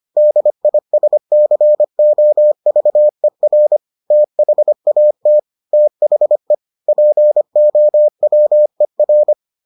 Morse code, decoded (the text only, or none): DISCOVER THAT THE POWER